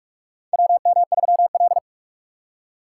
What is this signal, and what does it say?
Morse code, heard as a keyed tone, about 47 wpm, 700 Hz. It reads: WM3L
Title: USA Call Signs - 4 Characters - 3x Character Spacing - 45wpm